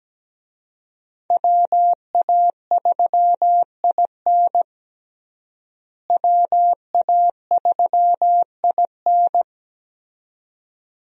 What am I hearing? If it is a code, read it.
Morse code, heard as a keyed tone, about 17 wpm, 700 Hz. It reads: WA3IN WA3IN
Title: USA Call Signs - 5 Characters - Repeated 2x - 17wpm